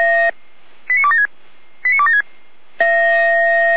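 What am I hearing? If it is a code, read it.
Five-tone selective call (ZVEI), heard as quick runs of five tones, tones of about 70 ms. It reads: 89276 89276
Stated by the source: ZVEI five-tone call 89169, received by SDR.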